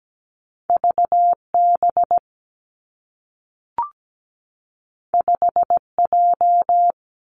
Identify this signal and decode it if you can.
Morse code, heard as a keyed tone, about 17 wpm, 700 Hz. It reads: VB 5J
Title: Two Challenging Letter-Number Combinations 17wpm